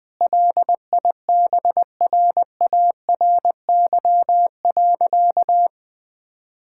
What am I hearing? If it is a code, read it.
Morse code, heard as a keyed tone, about 21 wpm, 700 Hz. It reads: LIBRARY.
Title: Difficult Sentences with Hints - 20wpm